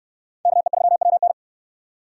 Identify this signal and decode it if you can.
Morse code, heard as a keyed tone, about 69 wpm, 700 Hz. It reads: KE3WN